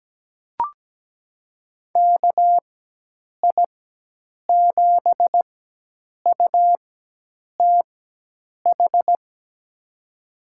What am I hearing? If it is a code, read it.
Morse code, heard as a keyed tone, about 17 wpm, spaced as for 8 wpm, 700 Hz. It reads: KI7UTH
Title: USA Call Signs - 6 Characters - 4x Character Spacing - 17wpm